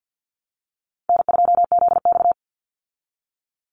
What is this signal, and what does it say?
Morse code, heard as a keyed tone, about 49 wpm, 700 Hz. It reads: N2ZX
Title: Call Signs Repeated 3x - 50wpm